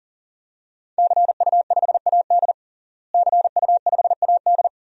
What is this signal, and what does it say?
Morse code, heard as a keyed tone, about 40 wpm, 700 Hz. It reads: CU5AD CU5AD
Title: DXCC-100 Call Signs - 5 Characters - Repeated 2x - 40wpm